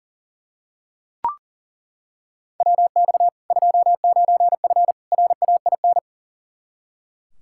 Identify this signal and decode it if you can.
Morse code, heard as a keyed tone, about 37 wpm, 700 Hz. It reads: WX 29F RAIN
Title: K1USN SST QSO Practice 40wpm